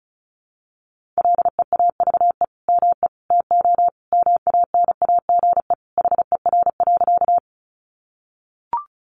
Morse code, heard as a keyed tone, about 34 wpm, 700 Hz. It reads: LEAVE ME TO MANAGE HER.
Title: Sentences from Top 1400 Words 35wpm